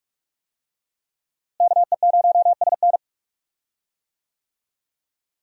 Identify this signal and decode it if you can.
Morse code, heard as a keyed tone, about 43 wpm, 700 Hz. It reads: KE0SN
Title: USA Call Signs - 5 Characters - Repeated 3x - 45wpm